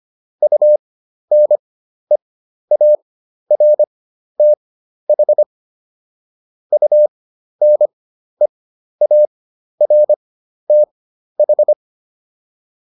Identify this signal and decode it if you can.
Morse code, heard as a keyed tone, about 25 wpm, spaced as for 12 wpm, 600 Hz. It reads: UNEARTH UNEARTH